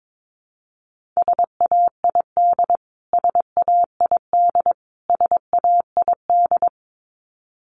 Morse code, heard as a keyed tone, about 22 wpm, 700 Hz. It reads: SAID SAID SAID
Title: Top 500 Words Repeated 22wpm